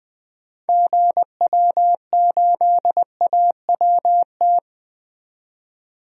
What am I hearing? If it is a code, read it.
Morse code, heard as a keyed tone, about 20 wpm, 700 Hz. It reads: GW8AWT
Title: DXCC-100 Call Signs - 6 Characters - 20wpm